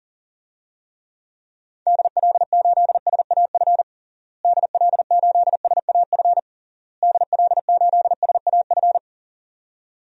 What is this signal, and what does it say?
Morse code, heard as a keyed tone, about 40 wpm, 700 Hz. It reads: DL8SAF DL8SAF DL8SAF
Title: DXCC-100 Call Signs - 6 Characters - Repeated 3x - 40wpm